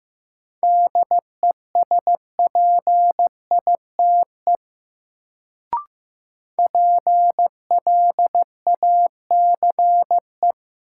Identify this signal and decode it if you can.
Morse code, heard as a keyed tone, about 15 wpm, 700 Hz. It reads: DESPITE PLACE
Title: Sets of 4 Words - New Word Drill - 15wpm